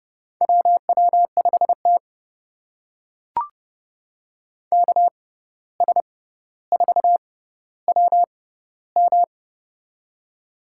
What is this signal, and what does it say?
Morse code, heard as a keyed tone, about 30 wpm, spaced as for 14 wpm, 700 Hz. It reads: WW5T KS4WM
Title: USA Call Signs - 6x Character Spacing - 30wpm